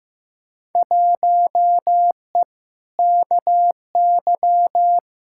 Morse code, heard as a keyed tone, about 15 wpm, 700 Hz. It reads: 1E KY
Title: Field Day Exchanges - 15wpm